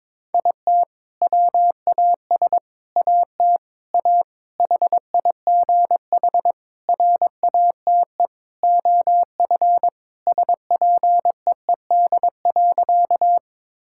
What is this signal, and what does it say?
Morse code, heard as a keyed tone, about 22 wpm, 700 Hz. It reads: IT WAS AT A HIGH RATE OF SPEED.
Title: Sentences from Top 1000 Words - Review Part 5 - 22wpm